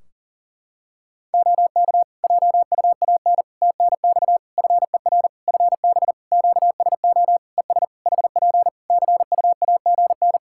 Morse code, heard as a keyed tone, about 40 wpm, 700 Hz. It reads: OK JUAN TNX FER FB QSO ES HP CUAGN
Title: Practice QSO - Exchange 4 - Ending - 40wpm